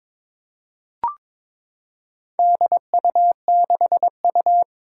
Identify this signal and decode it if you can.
Morse code, heard as a keyed tone, about 22 wpm, 700 Hz. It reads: DU6U
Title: Four Challenging Letter-Number Combinations 22wpm